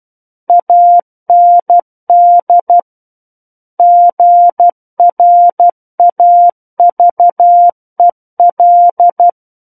Morse code, heard as a keyed tone, about 12 wpm, 700 Hz. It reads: AND GRAVEL
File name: Book One / 03-Chapter 3 - 12 WPM